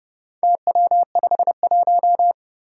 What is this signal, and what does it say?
Morse code, heard as a keyed tone, about 30 wpm, 700 Hz. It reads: TW51
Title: Four Challenging Letter-Number Combinations 30wpm